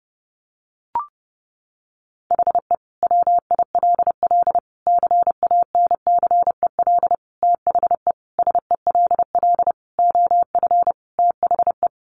Morse code, heard as a keyed tone, about 30 wpm, 700 Hz. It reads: HE WILL CANCEL THE SELL OF THE
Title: Sentences from Top 1200 Words 30wpm